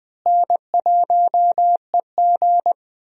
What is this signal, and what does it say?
Morse code, heard as a keyed tone, about 21 wpm, 700 Hz. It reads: N1EG
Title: USA Call Signs - 4 Characters - Repeated 3x - 20wpm